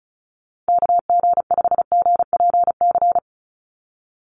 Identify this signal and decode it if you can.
Morse code, heard as a keyed tone, about 34 wpm, 700 Hz. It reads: KG5GPC
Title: USA Call Signs 35wpm